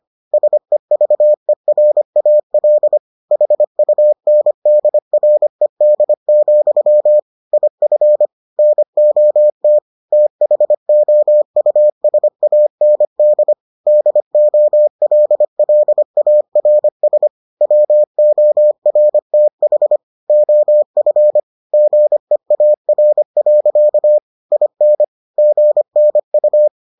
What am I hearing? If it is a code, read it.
Morse code, heard as a keyed tone, about 25 wpm, 600 Hz. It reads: SEVERAL HUNDRED, IF NOT THOUSAND DOLLARS WORTH OF GEAR. IN GNU